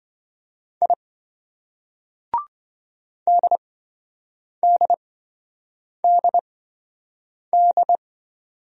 Morse code, heard as a keyed tone, about 27 wpm, spaced as for 7 wpm, 700 Hz. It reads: I DDDD